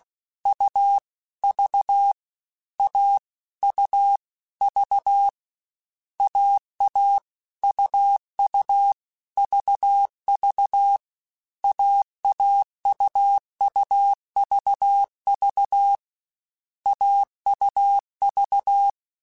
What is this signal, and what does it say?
Morse code, heard as a keyed tone, about 16 wpm, 800 Hz. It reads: U V A U V AA UU VV AAUUVV AUV